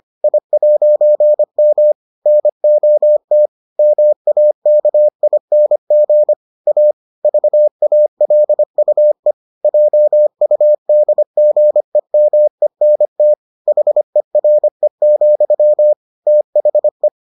Morse code, heard as a keyed tone, about 25 wpm, 600 Hz. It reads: I'M NOT MAKING A VALUE JUDGEMENT HERE, THE